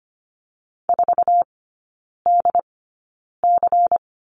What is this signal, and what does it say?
Morse code, heard as a keyed tone, about 25 wpm, spaced as for 9 wpm, 700 Hz. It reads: 4DC
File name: Lesson 17e - Letter u - Three Characters 25wpm